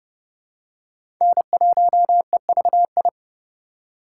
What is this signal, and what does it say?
Morse code, heard as a keyed tone, about 30 wpm, 700 Hz. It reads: N1EVI
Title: USA Call Signs - 5 Characters - 4x Character Spacing - 30wpm